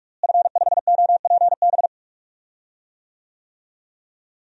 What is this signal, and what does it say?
Morse code, heard as a keyed tone, about 44 wpm, 700 Hz. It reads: F5OPB
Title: DXCC-100 Call Signs - 5 Characters - Repeated 2x - 45wpm